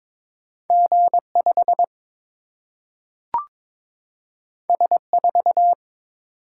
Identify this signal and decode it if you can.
Morse code, heard as a keyed tone, about 22 wpm, 700 Hz. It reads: G5 S4
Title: Single-Letter Single-Number 22wpm